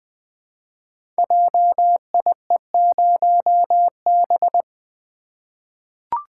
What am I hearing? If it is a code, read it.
Morse code, heard as a keyed tone, about 20 wpm, 700 Hz. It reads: JIE0B
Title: Code Groups - 5 Letter:Numbers - 20wpm